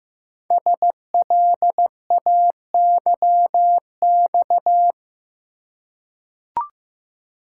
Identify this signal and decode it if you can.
Morse code, heard as a keyed tone, about 15 wpm, 700 Hz. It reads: SLAYX
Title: Code Groups - 5 Letters - 15wpm